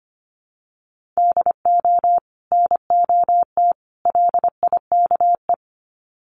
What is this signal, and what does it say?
Morse code, heard as a keyed tone, about 25 wpm, 700 Hz. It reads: DO NOT LIKE
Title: Sets of 3 Words 25wpm